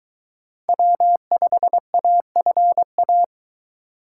Morse code, heard as a keyed tone, about 22 wpm, 700 Hz. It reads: W5AFA